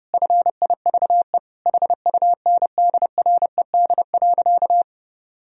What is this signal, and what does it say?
Morse code, heard as a keyed tone, about 32 wpm, 700 Hz. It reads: FIVE HUNDRED.
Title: Sentences from Top 1000 Words - Review Part 2 - 30wpm